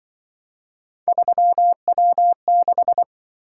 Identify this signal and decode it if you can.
Morse code, heard as a keyed tone, about 25 wpm, 700 Hz. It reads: 3W6
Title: Speed-Racing - Three Letter-Number Combinations - 20wpm